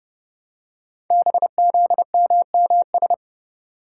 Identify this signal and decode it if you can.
Morse code, heard as a keyed tone, about 30 wpm, 700 Hz. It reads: BZMMS
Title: Code Groups - 5 Letters - 30wpm